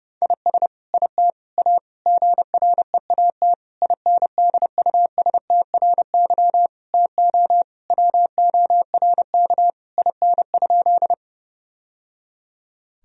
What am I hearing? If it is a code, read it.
Morse code, heard as a keyed tone, about 30 wpm, 700 Hz. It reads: IS IT A GREAT INDUSTRY TO WORK IN?
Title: Sentences from Top 1000 Words - Review Part 4 - 30wpm